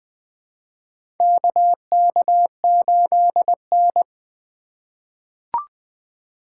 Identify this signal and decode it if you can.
Morse code, heard as a keyed tone, about 20 wpm, 700 Hz. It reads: KK8N
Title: USA Call Signs - 3x Character Spacing - 20wpm